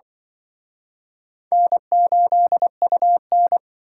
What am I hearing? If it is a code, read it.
Morse code, heard as a keyed tone, about 24 wpm, 700 Hz. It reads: N8UN
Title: Speed Racing - USA 1x2 Call Signs 20wpm